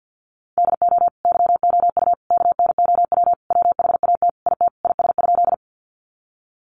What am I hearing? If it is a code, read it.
Morse code, heard as a keyed tone, about 50 wpm, 700 Hz. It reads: DO YOU KNOW WHAT IT IS?
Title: Sentences from Top 100 Words 50wpm